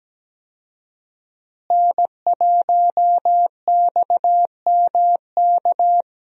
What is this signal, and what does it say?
Morse code, heard as a keyed tone, about 17 wpm, 700 Hz. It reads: N1XMK
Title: USA Call Signs - 5 Characters - Repeated 2x - 17wpm